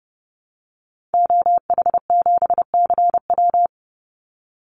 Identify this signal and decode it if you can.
Morse code, heard as a keyed tone, about 30 wpm, 700 Hz. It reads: OH7CW
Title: Call Signs Repeated 3x - 30wpm